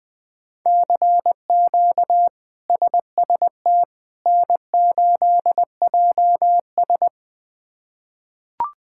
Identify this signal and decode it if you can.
Morse code, heard as a keyed tone, about 20 wpm, 700 Hz. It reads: CQ SST N8JS
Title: K1USN SST QSO Practice - 20wpm